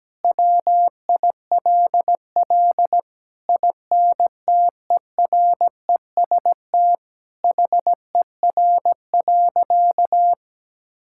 Morse code, heard as a keyed tone, about 17 wpm, 700 Hz. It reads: WILL INTEREST HER.
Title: Sentences from Top 1000 Words - Review Part 5 - 17wpm